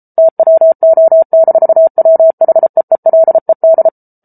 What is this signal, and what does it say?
Morse code, heard as a keyed tone, about 34 wpm, 650 Hz. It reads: TWO-WHEELED